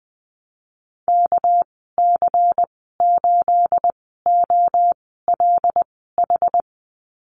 Morse code, heard as a keyed tone, about 20 wpm, spaced as for 14 wpm, 700 Hz. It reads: KC8OLH
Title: 245 Lesson 39g - Letter z - Call Signs 20wpm